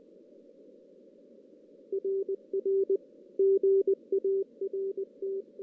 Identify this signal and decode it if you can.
Morse code, heard as a keyed tone, about 20 wpm, 390 Hz. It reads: RR GART
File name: 2016 JAN CW NAQP / 2016 Jan naqp k5ot - 0-6